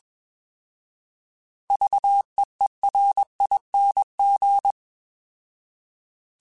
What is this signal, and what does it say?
Morse code, heard as a keyed tone, about 21 wpm, 800 Hz. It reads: VEERING
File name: Alphabetical / Alphabetical mp3s / Vs @20